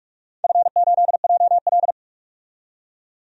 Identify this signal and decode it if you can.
Morse code, heard as a keyed tone, about 46 wpm, 700 Hz. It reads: F8JL
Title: DXCC-100 Call Signs - 4 Characters - 45wpm